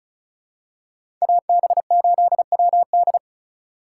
Morse code, heard as a keyed tone, about 34 wpm, 700 Hz. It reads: AB8WD
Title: USA Call Signs - 5 Characters - 4x Character Spacing - 35wpm